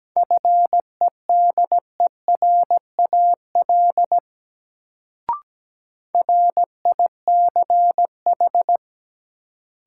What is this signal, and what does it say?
Morse code, heard as a keyed tone, about 17 wpm, 700 Hz. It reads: FEDERAL RICH
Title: Sentences from Top 1100 Words - New Word Drill Alt - 17wpm